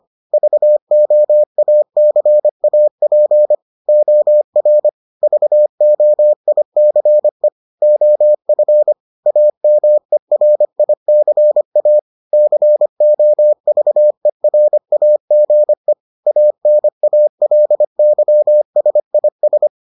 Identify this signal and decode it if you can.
Morse code, heard as a keyed tone, about 25 wpm, 600 Hz. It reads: VOACAP OR VOICE OF AMERICA COVERAGE ANALYSIS